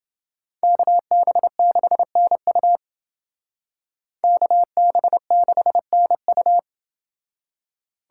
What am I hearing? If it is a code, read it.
Morse code, heard as a keyed tone, about 29 wpm, 700 Hz. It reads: KB6NU KB6NU